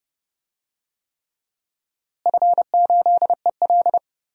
Morse code, heard as a keyed tone, about 30 wpm, 700 Hz. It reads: F8EL
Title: DXCC-100 Call Signs - 4 Characters - Repeated 3x - 30wpm